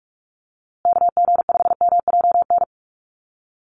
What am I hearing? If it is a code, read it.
Morse code, heard as a keyed tone, about 44 wpm, 700 Hz. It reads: KG5MJN
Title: USA Call Signs Repeated 2x 45wpm